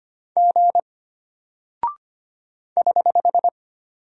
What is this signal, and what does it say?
Morse code, heard as a keyed tone, about 25 wpm, 700 Hz. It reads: G <HH>
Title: Single Character (Letter:Number:Punctuation:Prosign) Alt 25wpm